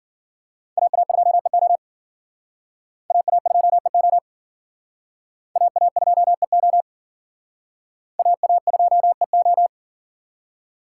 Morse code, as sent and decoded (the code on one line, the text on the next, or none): .- .- ..--- . --- / .- .- ..--- . --- / .- .- ..--- . --- / .- .- ..--- . ---
AA2EO AA2EO AA2EO AA2EO